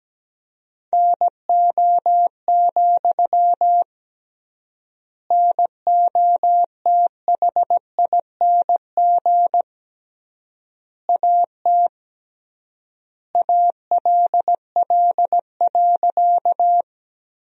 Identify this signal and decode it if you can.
Morse code, heard as a keyed tone, about 17 wpm, 700 Hz. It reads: NO, NOTHING AT ALL.
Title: 3x Word-Spacing - Sentences from Top 500 Words - 17wpm